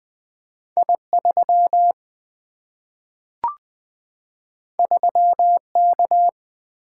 Morse code, heard as a keyed tone, about 20 wpm, 700 Hz. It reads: I3 3K
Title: Single-Letter Single-Number 20wpm